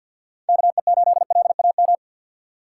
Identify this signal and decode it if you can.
Morse code, heard as a keyed tone, about 52 wpm, 690 Hz. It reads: KE8LAM
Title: USA Call Signs - 6 Characters - 3x Character Spacing - 50wpm